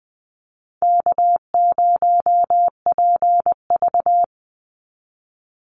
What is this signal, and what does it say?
Morse code, heard as a keyed tone, about 20 wpm, 700 Hz. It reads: K0PV
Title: Easy Call Signs - 20wpm